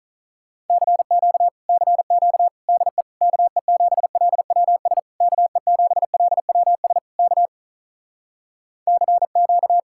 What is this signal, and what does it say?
Morse code, heard as a keyed tone, about 46 wpm, 700 Hz. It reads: CQ CQ DE KE7LWS KE7LWS K CQ